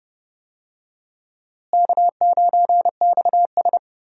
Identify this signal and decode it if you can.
Morse code, heard as a keyed tone, about 31 wpm, 700 Hz. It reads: K9XS
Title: USA Call Signs - 4 Characters - Repeated 3x - 30wpm